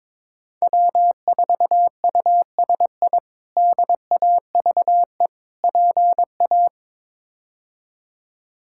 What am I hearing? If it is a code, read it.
Morse code, heard as a keyed tone, about 21 wpm, 700 Hz. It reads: W4USI DAVE PA
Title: K1USN SST QSO Practice - 22wpm